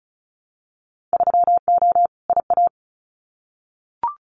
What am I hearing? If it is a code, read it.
Morse code, heard as a keyed tone, about 34 wpm, 700 Hz. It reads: 3O IA
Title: Winter Field Day Exchanges 35wpm